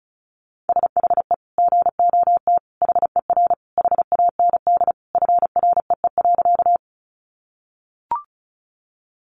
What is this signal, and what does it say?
Morse code, heard as a keyed tone, about 34 wpm, 700 Hz. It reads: SHE GOT HER HAND FREE.
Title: Sentences from Top 500 Words 35wpm